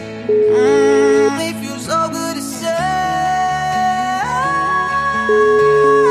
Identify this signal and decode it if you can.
Ringback tone (Europe, 425 Hz), with noise.